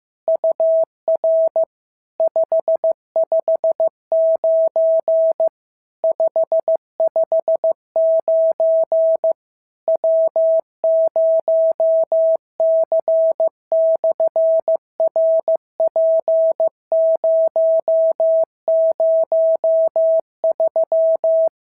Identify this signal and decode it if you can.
Morse code, heard as a keyed tone, about 15 wpm, 650 Hz. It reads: UR 559 559 W0C/RP003
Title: SOTA QSO Practice 15wpm